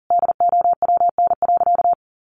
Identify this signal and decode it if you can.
Morse code, heard as a keyed tone, about 40 wpm, 700 Hz. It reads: DOWN.
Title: Sentences from Top 1200 Words 40wpm